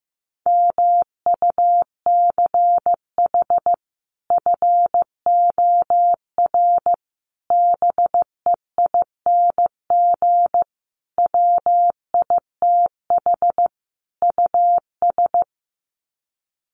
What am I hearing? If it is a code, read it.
Morse code, heard as a keyed tone, about 15 wpm, 700 Hz. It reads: MUCH FOR BEING WITH US
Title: Sets of 5 Words 15wpm